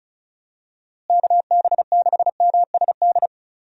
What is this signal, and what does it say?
Morse code, heard as a keyed tone, about 34 wpm, 700 Hz. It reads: KB6MSD
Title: USA Call Signs - 6 Characters - Repeated 3x - 35wpm